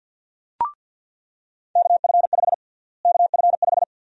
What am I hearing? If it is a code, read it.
Morse code, heard as a keyed tone, about 49 wpm, 700 Hz. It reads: KF5 KF5
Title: USA Call Sign Prefixes - Repeated 2x - 50wpm